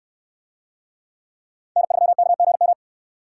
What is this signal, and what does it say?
Morse code, heard as a keyed tone, about 68 wpm, 700 Hz. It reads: N3KGM